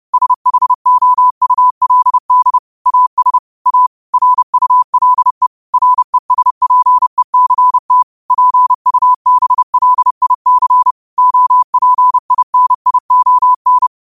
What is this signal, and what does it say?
Morse code, heard as a keyed tone, about 30 wpm, 1000 Hz. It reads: SHOULD AS A RULE RESPECT PUBLIC OPINION
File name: qotdfiles / 30wpm